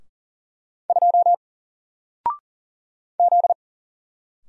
Morse code, heard as a keyed tone, about 40 wpm, 700 Hz. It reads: 2 Z
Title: Single Character (Letter:Number:Punctuation:Prosign) 40wpm